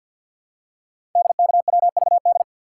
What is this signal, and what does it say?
Morse code, heard as a keyed tone, about 48 wpm, 690 Hz. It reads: DKWVD
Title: Code Groups - 5 Letters - 50wpm